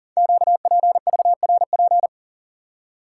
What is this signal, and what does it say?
Morse code, heard as a keyed tone, about 40 wpm, 700 Hz. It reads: QPVRP
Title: Code Groups - 5 Letters - 40wpm